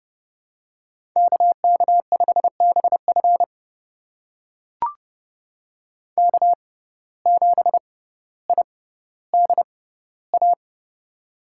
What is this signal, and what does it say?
Morse code, heard as a keyed tone, about 30 wpm, spaced as for 15 wpm, 700 Hz. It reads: KK5BF K7IDA
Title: USA Call Signs - 5 Characters - 6x Character Spacing - 30wpm